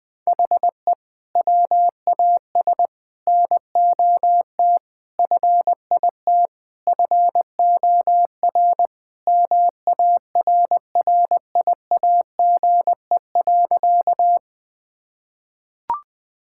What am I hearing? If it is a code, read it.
Morse code, heard as a keyed tone, about 20 wpm, 700 Hz. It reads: HE WAS NOT FIT FOR MARRIAGE.
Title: Sentences from Top 1000 Words - Review Part 5 - 20wpm